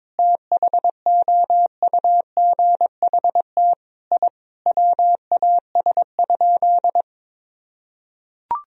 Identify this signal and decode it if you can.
Morse code, heard as a keyed tone, about 22 wpm, 700 Hz. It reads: THOUGHT I WAS?
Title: Sentences from Top 1000 Words - Review Part 3 - 22wpm